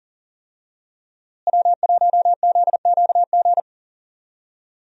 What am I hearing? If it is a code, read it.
Morse code, heard as a keyed tone, about 40 wpm, 700 Hz. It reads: W1ZQG
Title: USA Call Signs - 5 Characters - Repeated 3x - 40wpm